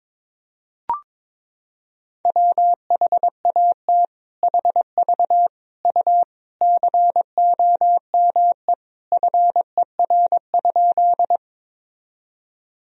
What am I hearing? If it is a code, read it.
Morse code, heard as a keyed tone, about 22 wpm, 700 Hz. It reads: WHAT HV U COME FER?
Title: Sentences from Top 100 Words and Abbreviations 22wpm